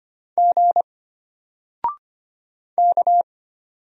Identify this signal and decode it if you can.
Morse code, heard as a keyed tone, about 25 wpm, 700 Hz. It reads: G K
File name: Single-Letter-Number-25wpm